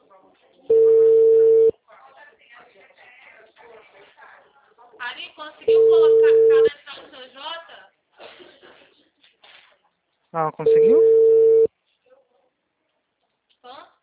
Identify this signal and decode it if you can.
Ringback tone (China, 450 Hz), with speech.
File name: out- -2024